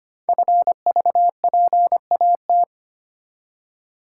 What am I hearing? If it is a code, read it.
Morse code, heard as a keyed tone, about 25 wpm, 700 Hz. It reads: FVPAT
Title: Code Groups - 5 Letters - 25wpm